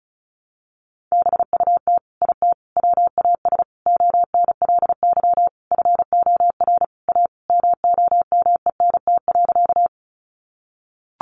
Morse code, heard as a keyed tone, about 34 wpm, 700 Hz. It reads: BUT IT WAS ONLY FOR A MOMENT.
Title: Sentences from Top 1000 Words 35wpm